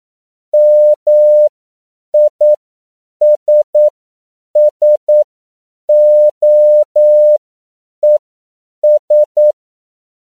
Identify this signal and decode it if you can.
Morse code, heard as a keyed tone, about 9 wpm, spaced as for 7 wpm, 600 Hz. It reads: MISSOES
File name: CW / 10PPM